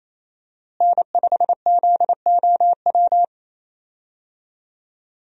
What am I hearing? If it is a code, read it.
Morse code, heard as a keyed tone, about 28 wpm, 700 Hz. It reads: N5ZOW
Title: USA Call Signs - 5 Characters - Repeated 2x - 28wpm